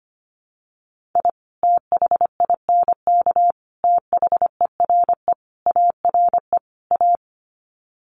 Morse code, heard as a keyed tone, about 25 wpm, 700 Hz. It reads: I THINK THERE ARE A
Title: Sets of 5 Words Encore2 25wpm